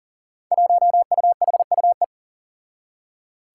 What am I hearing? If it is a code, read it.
Morse code, heard as a keyed tone, about 40 wpm, 700 Hz. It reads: 1UHUE
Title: Five Challenging Letter-Number Combinations 40wpm